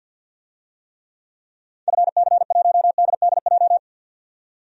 Morse code, heard as a keyed tone, about 52 wpm, 700 Hz. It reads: FC1DDJ
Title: DXCC-100 Call Signs - 6 Characters - 50wpm